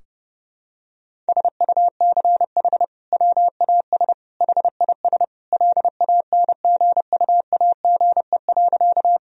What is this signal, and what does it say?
Morse code, heard as a keyed tone, about 30 wpm, 700 Hz. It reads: SUCH WAS HIS LANGUAGE.
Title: Sentences from Top 1000 Words - Review Part 3 - 30wpm